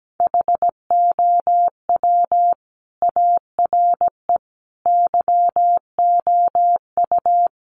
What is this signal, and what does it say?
Morse code, heard as a keyed tone, about 17 wpm, 700 Hz. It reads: HOW ARE YOU
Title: Sets of 5 Words 17wpm